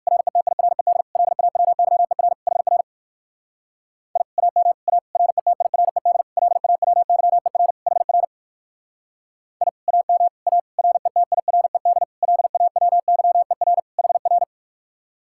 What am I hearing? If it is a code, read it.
Morse code, heard as a keyed tone, about 54 wpm, 700 Hz. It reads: RETIRED LAWYER HR I AM A RETIRED LAWYER HR I AM A RETIRED LAWYER HR
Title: Speed-Racing Common QSO Phrases - 45wpm